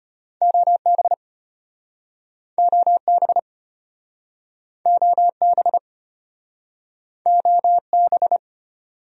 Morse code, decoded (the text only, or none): OB OB OB OB